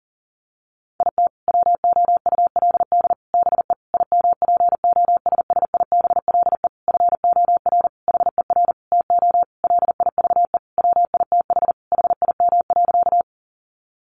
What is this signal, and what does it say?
Morse code, heard as a keyed tone, about 40 wpm, 700 Hz. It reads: IT WOULD BE IMPOSSIBLE FOR HER TO LIVE WITH HIM.
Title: Sentences from Top 1100 Words 40wpm